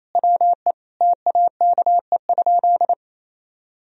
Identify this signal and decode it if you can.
Morse code, heard as a keyed tone, about 28 wpm, 700 Hz. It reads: WE TAKE?
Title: Sentences from Top 1000 Words - Review Part 4 - 28wpm